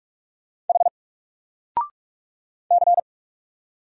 Morse code, heard as a keyed tone, about 46 wpm, 700 Hz. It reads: H C
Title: Single Letter-Number Alt 45wpm